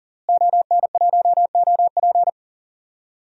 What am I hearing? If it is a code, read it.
Morse code, heard as a keyed tone, about 39 wpm, 700 Hz. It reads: ON1OP
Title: DXCC-100 Call Signs - 5 Characters - Repeated 2x - 40wpm